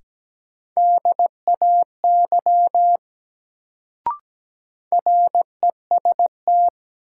Morse code, heard as a keyed tone, about 17 wpm, 700 Hz. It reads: DAY REST
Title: Sets of 2 Words - New Word Drill - 17wpm